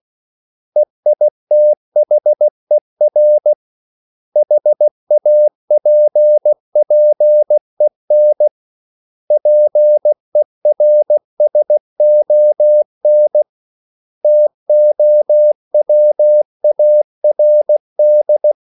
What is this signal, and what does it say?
Morse code, heard as a keyed tone, about 16 wpm, 600 Hz. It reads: EITHER HAPPEN PERSON TOWARD
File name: CW / Practice / 500MostCommonEnglishWords16